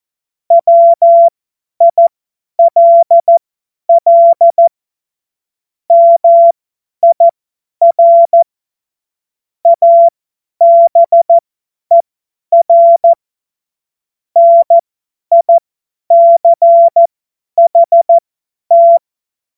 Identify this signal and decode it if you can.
Morse code, heard as a keyed tone, about 14 wpm, spaced as for 10 wpm, 680 Hz. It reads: WILL MIR ABER NICHT
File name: Morsekurs / Witze / Witz 06. Juli 2021, 50 F70